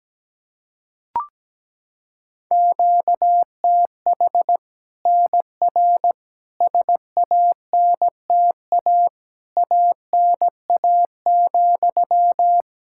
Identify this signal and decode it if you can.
Morse code, heard as a keyed tone, about 17 wpm, 700 Hz. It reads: QTH NR SANTA ANA,
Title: Practice QSO - Exchange 1 - Beginning - 17wpm